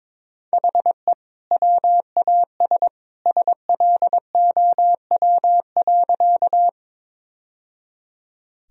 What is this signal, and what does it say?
Morse code, heard as a keyed tone, about 22 wpm, 700 Hz. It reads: HE WAS SLOW.
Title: Sentences from Top 1000 Words - Review Part 2 - 22wpm